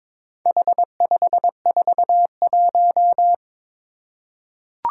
Morse code, heard as a keyed tone, about 22 wpm, 700 Hz. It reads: H541
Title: Four Challenging Letter-Number Combinations 22wpm